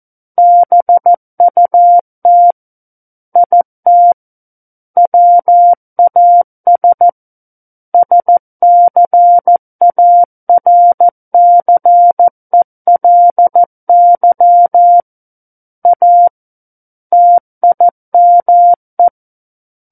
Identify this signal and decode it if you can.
Morse code, heard as a keyed tone, about 14 wpm, 700 Hz. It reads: BUT IT WAS SCARCELY A TIME
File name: Book One / 05-Chapter 5 - 14 WPM